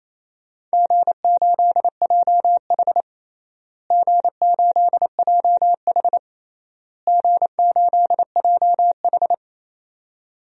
Morse code, heard as a keyed tone, about 28 wpm, 700 Hz. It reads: G8JH G8JH G8JH
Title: DXCC-100 Call Signs - 4 Characters - Repeated 3x - 28wpm